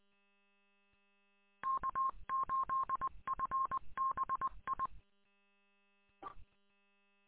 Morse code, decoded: K8FBI

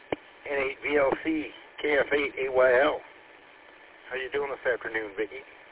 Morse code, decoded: none

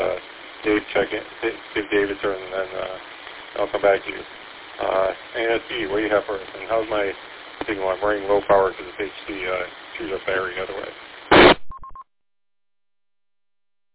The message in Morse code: none

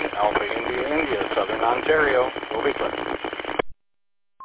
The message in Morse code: none

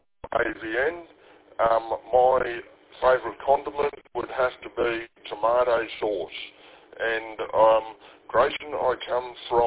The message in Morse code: none

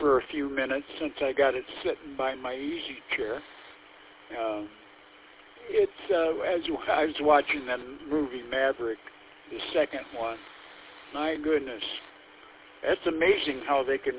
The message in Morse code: none

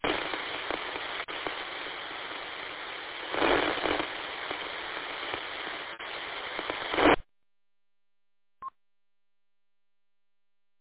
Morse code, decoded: none

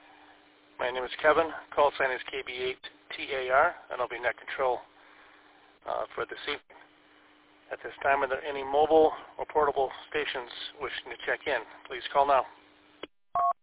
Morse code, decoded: none

